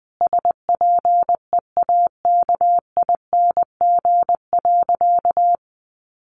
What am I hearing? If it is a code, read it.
Morse code, heard as a keyed tone, about 20 wpm, 700 Hz. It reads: SPEAKING.
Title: A Sound Of Thunder 20wpm